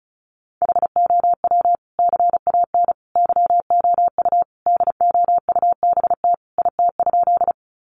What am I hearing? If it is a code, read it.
Morse code, heard as a keyed tone, about 34 wpm, 700 Hz. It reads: HOW CAN YOU DOUBT IT?
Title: Sentences from Top 1500 Words 35wpm